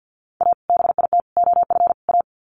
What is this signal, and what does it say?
Morse code, heard as a keyed tone, about 50 wpm, 700 Hz. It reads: A BIT OF A
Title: Sets of 4 Words 50wpm